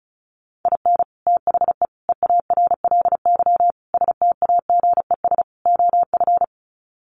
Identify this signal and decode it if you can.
Morse code, heard as a keyed tone, about 34 wpm, 700 Hz. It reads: IN THE EARLY STAGES OF
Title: Sets of 5 Words Encore 35wpm